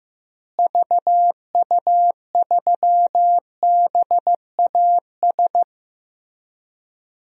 Morse code, heard as a keyed tone, about 15 wpm, 700 Hz. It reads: VU3BAS
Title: DXCC-100 Call Signs - 6 Characters - Repeated 2x - 15wpm